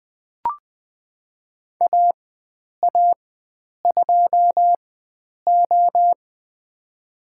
Morse code, heard as a keyed tone, about 20 wpm, spaced as for 9 wpm, 700 Hz. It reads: AA2O
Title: USA Call Signs - 4x Character Spacing - 20wpm